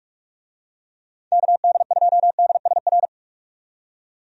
Morse code, heard as a keyed tone, about 46 wpm, 700 Hz. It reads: KD2DSR
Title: USA Call Signs - 6 Characters - 45wpm